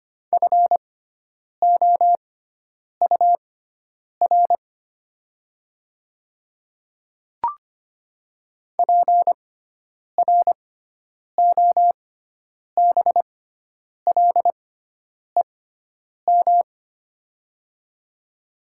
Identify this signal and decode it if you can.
Morse code, heard as a keyed tone, about 25 wpm, spaced as for 9 wpm, 700 Hz. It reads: FOUR PROBLEM
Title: 6x Character Spacing - Top 500 Words - 25wpm